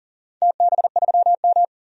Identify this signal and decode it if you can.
Morse code, heard as a keyed tone, about 40 wpm, 700 Hz. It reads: TB3M
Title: DXCC-100 Call Signs - 4 Characters - 40wpm